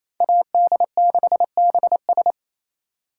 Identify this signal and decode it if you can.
Morse code, heard as a keyed tone, about 35 wpm, 700 Hz. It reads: AD6BS
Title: Speed Racing - USA Call Signs 28wpm